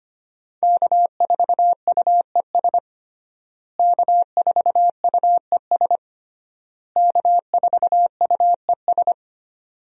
Morse code, heard as a keyed tone, about 25 wpm, 700 Hz. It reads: K4UES K4UES K4UES
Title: USA Call Signs - 5 Characters - Repeated 3x - 25wpm